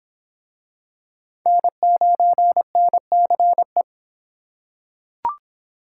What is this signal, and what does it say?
Morse code, heard as a keyed tone, about 26 wpm, 700 Hz. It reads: N9NCE